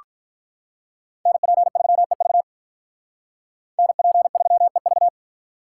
Morse code, heard as a keyed tone, about 43 wpm, 700 Hz. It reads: NP3EV NP3EV
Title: Speed Racing - USA 2x2 Call Signs 35wpm